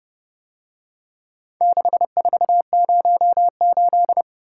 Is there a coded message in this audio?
Yes: Morse code from 1.6 to 4.2 s, heard as a keyed tone, about 29 wpm, 700 Hz. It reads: 6408